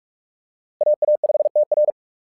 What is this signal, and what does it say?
Morse code, heard as a keyed tone, about 44 wpm, 600 Hz. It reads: AA5TR
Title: SOTA QSO Practice 45wpm